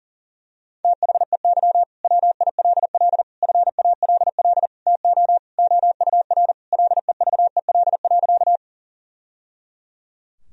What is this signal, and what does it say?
Morse code, heard as a keyed tone, about 40 wpm, 690 Hz. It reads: THEY WILL FALL TO OUR LEVEL.
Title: Sentences from Top 1000 Words - Review Part 4 - 40wpm